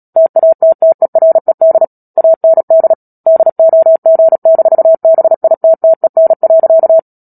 Morse code, heard as a keyed tone, about 37 wpm, 650 Hz. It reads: TATTERED AND DOG-BITTEN.
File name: Book Two / 10-Chapter 10 - Epilogue - 36 WPM